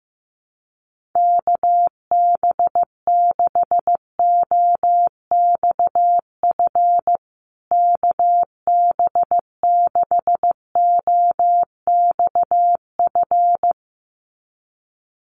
Morse code, heard as a keyed tone, about 15 wpm, 700 Hz. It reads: KB6OXF KB6OXF
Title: USA Call Signs Repeated 2x 15wpm